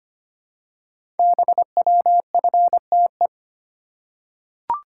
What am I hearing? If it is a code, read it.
Morse code, heard as a keyed tone, about 25 wpm, 700 Hz. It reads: BWFTE
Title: Code Groups - 5 Letters - 25wpm